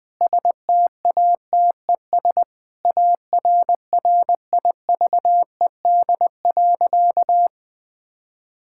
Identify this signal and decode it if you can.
Morse code, heard as a keyed tone, about 20 wpm, 700 Hz. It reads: STATES ARRIVED.
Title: Sentences with 6-Word Phrases - 20wpm